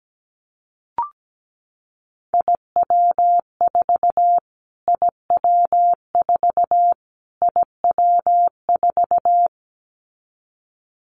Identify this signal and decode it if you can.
Morse code, heard as a keyed tone, about 17 wpm, 700 Hz. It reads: IW4 IW4 IW4
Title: Callsign Prefixes for Top 25 DXCC - Repeated 3x - 17wpm